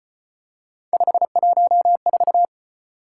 Morse code, heard as a keyed tone, about 35 wpm, 700 Hz. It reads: 514